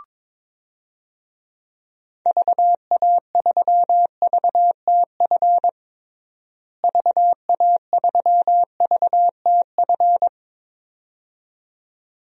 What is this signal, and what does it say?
Morse code, heard as a keyed tone, about 22 wpm, 700 Hz. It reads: VA3VTF VA3VTF
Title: DXCC-100 Call Signs - 6 Characters - Repeated 2x - 22wpm